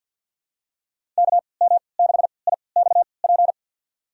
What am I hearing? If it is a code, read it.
Morse code, heard as a keyed tone, about 53 wpm, spaced as for 29 wpm, 700 Hz. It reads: KM6IXP
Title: USA Call Signs - 3x Character Spacing - 50wpm